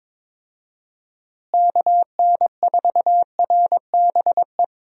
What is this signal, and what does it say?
Morse code, heard as a keyed tone, about 22 wpm, 700 Hz. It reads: KN4RBE
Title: USA Call Signs - 6 Characters - Repeated 3x - 22wpm